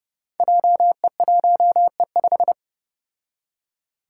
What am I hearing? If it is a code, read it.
Morse code, heard as a keyed tone, about 30 wpm, 700 Hz. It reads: JE1E5
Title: Five Challenging Letter-Number Combinations 30wpm